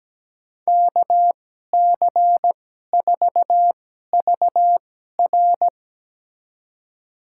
Morse code, heard as a keyed tone, about 17 wpm, spaced as for 12 wpm, 700 Hz. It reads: KC4VR
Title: USA Call Signs - 5 Characters - 2x Character Spacing - 17wpm